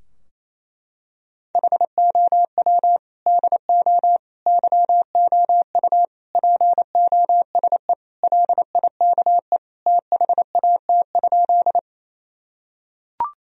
Morse code, heard as a keyed tone, about 27 wpm, 700 Hz. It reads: HOW DO YOU POSE LIKE THAT?
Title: Sentences from Top 1000 Words - Review Part 2 - 28wpm